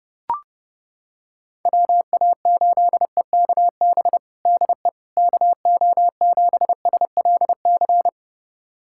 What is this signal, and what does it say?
Morse code, heard as a keyed tone, about 30 wpm, 700 Hz. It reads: WA8EKB DE KO7SLC
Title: USA Callsign DE Callsign - 6 Characters - 30wpm